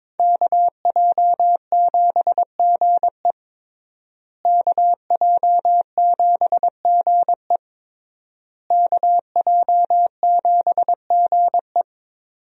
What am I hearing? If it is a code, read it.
Morse code, heard as a keyed tone, about 22 wpm, 700 Hz. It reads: KJ7GE KJ7GE KJ7GE
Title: USA Call Signs - 5 Characters - Repeated 3x - 22wpm